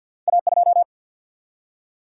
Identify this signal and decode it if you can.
Morse code, heard as a keyed tone, about 48 wpm, 700 Hz. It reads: A2
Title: Single-Letter Single-Number 50wpm